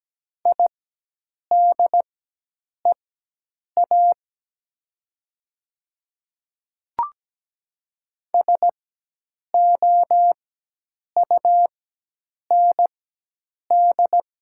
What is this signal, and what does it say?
Morse code, heard as a keyed tone, about 17 wpm, spaced as for 8 wpm, 700 Hz. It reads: IDEA SOUND
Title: 4x Character Spacing - Top 300 Words - 17wpm